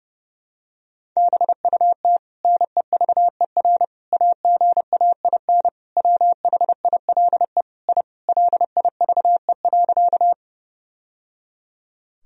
Morse code, heard as a keyed tone, about 30 wpm, 700 Hz. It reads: BUT NEVER AGAIN WHILE I LIVE.
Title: Sentences from Top 1000 Words - Review Part 1 - 30wpm